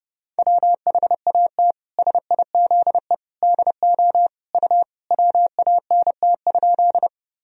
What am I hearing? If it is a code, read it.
Morse code, heard as a keyed tone, about 30 wpm, 700 Hz. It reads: WHAT SIZE DO U WANT?
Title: Sentences from Top 400 Words and Abbreviations 30wpm